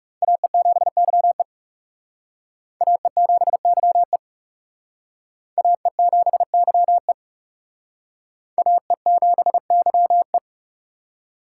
Morse code, heard as a keyed tone, about 37 wpm, 700 Hz. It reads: AE7YE AE7YE AE7YE AE7YE
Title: Speed Racing - USA 2x2 Call Signs 30wpm